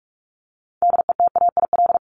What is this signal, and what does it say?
Morse code, heard as a keyed tone, about 44 wpm, 700 Hz. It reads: DETAIL
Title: Top 1251-1500 Words 45wpm